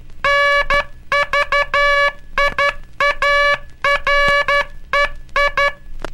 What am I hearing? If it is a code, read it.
Morse code, heard as a keyed tone, about 12 wpm, 1700 Hz, with noise. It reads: NVIAREI
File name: cursos / Curso CW mp3 / CW Aula - 18